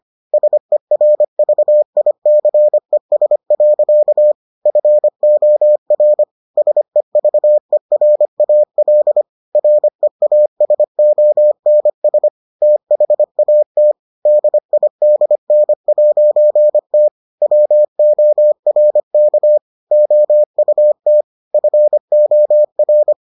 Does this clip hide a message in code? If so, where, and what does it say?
Yes: Morse code from 0.3 to 23.1 s, heard as a keyed tone, about 25 wpm, 600 Hz. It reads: SERVICES. FOR SEVERAL REASONS THAT DIDN'T WORK OUT FOR